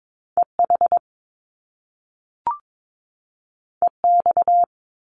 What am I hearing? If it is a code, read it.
Morse code, heard as a keyed tone, about 22 wpm, 700 Hz. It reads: EH EX
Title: Top Two Letter Combinations in Words 22wpm